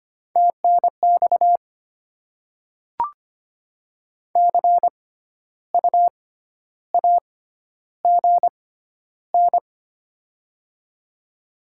Morse code, heard as a keyed tone, about 25 wpm, spaced as for 12 wpm, 700 Hz. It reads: TNX CU AGN
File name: Top 20 QSO Elements - 6x Character Spacing - 25wpm